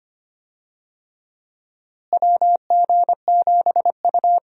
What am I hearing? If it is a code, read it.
Morse code, heard as a keyed tone, about 25 wpm, 700 Hz. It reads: WG7U
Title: USA Call Signs - 4 Characters - Repeated 3x - 25wpm